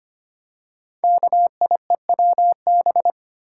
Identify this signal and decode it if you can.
Morse code, heard as a keyed tone, about 25 wpm, 700 Hz. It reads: KIEWB
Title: Code Groups - 5 Letters - 25wpm